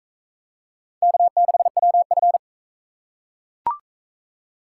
Morse code, heard as a keyed tone, about 38 wpm, 700 Hz. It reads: K6WF